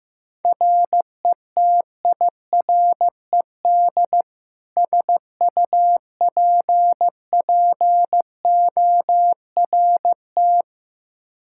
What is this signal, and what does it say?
Morse code, heard as a keyed tone, about 15 wpm, 700 Hz. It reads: RETIRED SUPPORT
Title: K1USN SST QSO Practice 15wpm